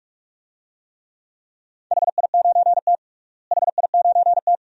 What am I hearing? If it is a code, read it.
Morse code, heard as a keyed tone, about 42 wpm, 700 Hz. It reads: HI9T HI9T
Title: DXCC-100 Call Signs - 4 Characters - Repeated 2x - 45wpm